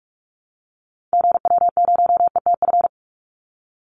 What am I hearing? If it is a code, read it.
Morse code, heard as a keyed tone, about 44 wpm, 700 Hz. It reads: GW0ETF
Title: Call Signs - 45wpm